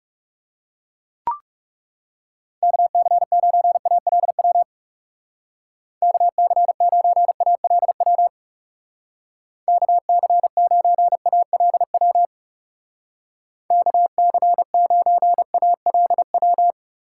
Morse code, heard as a keyed tone, about 35 wpm, 700 Hz. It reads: KC9ALW KC9ALW KC9ALW KC9ALW
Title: Speed Racing - USA Call Signs 30wpm